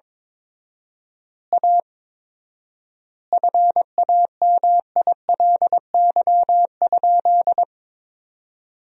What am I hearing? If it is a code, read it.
Morse code, heard as a keyed tone, about 22 wpm, 700 Hz. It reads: A FAMILY?
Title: 4x Word-Spacing - Sentences from Top 400 Words - 22wpm